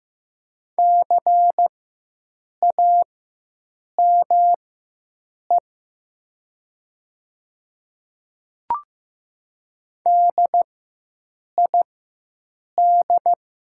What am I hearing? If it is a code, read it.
Morse code, heard as a keyed tone, about 15 wpm, spaced as for 7 wpm, 700 Hz. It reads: CAME DID
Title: 4x Character Spacing - Top 200 Words - 15wpm